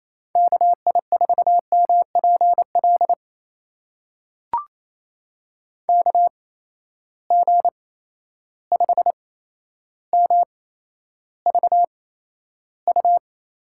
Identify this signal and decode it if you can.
Morse code, heard as a keyed tone, about 28 wpm, spaced as for 12 wpm, 700 Hz. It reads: KI4MPL KG5MVU
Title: USA Call Signs - 6 Characters - 8x Character Spacing - 28wpm